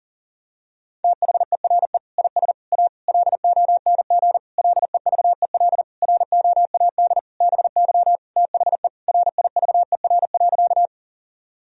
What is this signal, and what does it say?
Morse code, heard as a keyed tone, about 40 wpm, 700 Hz. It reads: THERE IS A LONG LEVEL ROAD BY THE RIVER.
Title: Sentences from Top 1000 Words - Review Part 4 - 40wpm